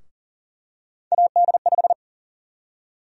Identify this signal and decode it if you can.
Morse code, heard as a keyed tone, about 39 wpm, 700 Hz. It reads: AD5
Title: USA Call Sign Prefixes - Repeated 2x - 40wpm